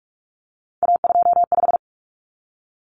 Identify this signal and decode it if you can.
Morse code, heard as a keyed tone, about 44 wpm, 700 Hz. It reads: A25
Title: Common Cut Numbers - 3-Digit 45wpm